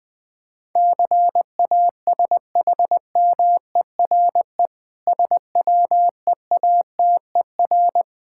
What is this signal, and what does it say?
Morse code, heard as a keyed tone, about 20 wpm, 700 Hz. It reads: CASHMERE SWEATER
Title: Difficult Words with Hints - 20wpm